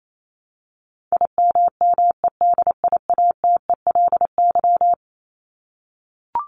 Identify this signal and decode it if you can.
Morse code, heard as a keyed tone, about 28 wpm, 700 Hz. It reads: IMMEDIATELY
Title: LY Suffix - Top 100 Words with Base Words 28wpm